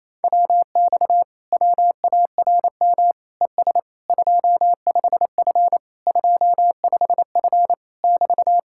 Morse code, heard as a keyed tone, about 29 wpm, 700 Hz. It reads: WX WARM ES 25F 25F <BT>
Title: Speed-Racing Common QSO Phrases - 28wpm